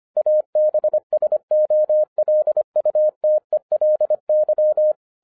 Morse code, heard as a keyed tone, about 25 wpm, 610 Hz. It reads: ABSOLUTELY